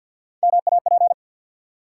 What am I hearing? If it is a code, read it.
Morse code, heard as a keyed tone, about 53 wpm, 700 Hz. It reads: MAP
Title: Three Letter Words - 50wpm